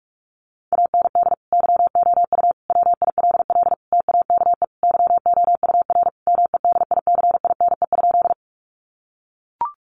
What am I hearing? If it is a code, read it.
Morse code, heard as a keyed tone, about 45 wpm, 700 Hz. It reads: AND YOU WILL TAKE YOUR MEDICINE?
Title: Sentences from Top 900 Words 45wpm